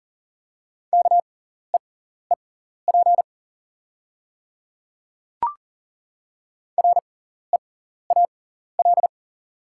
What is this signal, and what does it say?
Morse code, heard as a keyed tone, about 40 wpm, spaced as for 14 wpm, 700 Hz. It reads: KEEP REAL
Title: 6x Character Spacing - Top 300 Words - 40wpm